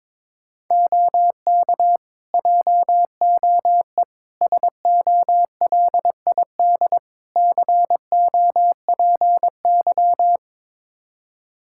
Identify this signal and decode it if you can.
Morse code, heard as a keyed tone, about 22 wpm, 700 Hz. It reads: OK JOE SOLID COPY
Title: Practice QSO - Exchange 3 - More Info - 22wpm